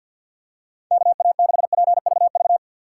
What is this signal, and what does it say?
Morse code, heard as a keyed tone, about 53 wpm, 700 Hz. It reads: KA6PVV
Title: USA Call Signs - 3x Character Spacing - 50wpm